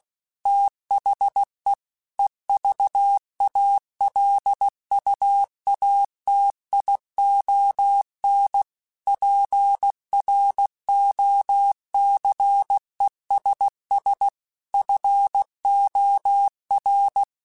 Morse code, heard as a keyed tone, about 16 wpm, 800 Hz. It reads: THE EVALUATION PROCESS FOR
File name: Technical Subjects / Technical Subjects-mp3s / AND Gate @15